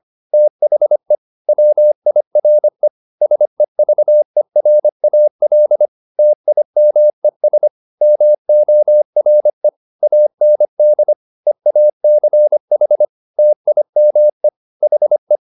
Morse code, heard as a keyed tone, about 25 wpm, 600 Hz. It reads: THE WIRE SEVERAL TIMES MORE AND EACH TIME HE